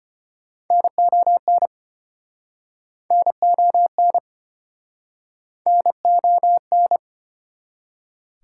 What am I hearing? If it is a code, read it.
Morse code, heard as a keyed tone, about 33 wpm, 700 Hz. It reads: NON NON NON